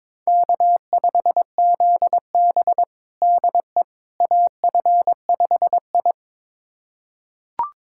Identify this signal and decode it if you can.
Morse code, heard as a keyed tone, about 22 wpm, 700 Hz. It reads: K5ZB DE AF5I
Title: USA Callsign DE Callsign - 4 Characters - 22wpm